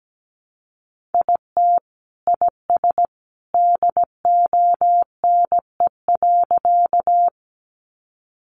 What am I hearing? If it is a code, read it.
Morse code, heard as a keyed tone, about 17 wpm, 700 Hz. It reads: IT IS DONE.
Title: Sentences from Top 500 Words 17wpm